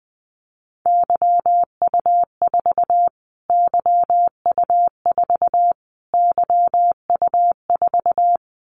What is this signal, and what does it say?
Morse code, heard as a keyed tone, about 20 wpm, 700 Hz. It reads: YU4 YU4 YU4
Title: Callsign Prefixes for Top 25 DXCC - Repeated 3x - 20wpm